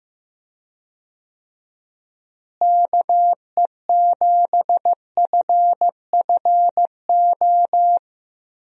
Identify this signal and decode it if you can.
Morse code, heard as a keyed tone, about 15 wpm, 700 Hz. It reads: KE7FFO
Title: USA Call Signs - 6 Characters - Repeated 3x - 15wpm